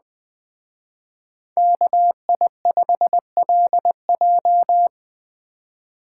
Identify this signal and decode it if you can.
Morse code, heard as a keyed tone, about 20 wpm, 700 Hz. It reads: KI5LJ
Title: Speed Racing - USA 2x2 Call Signs 17wpm